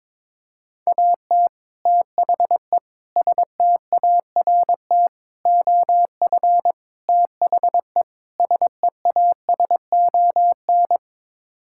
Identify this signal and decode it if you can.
Morse code, heard as a keyed tone, about 22 wpm, 700 Hz. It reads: AT THE START OF THE SEASON
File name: practice / 22wpm / Sets of 6 Words - 22wpm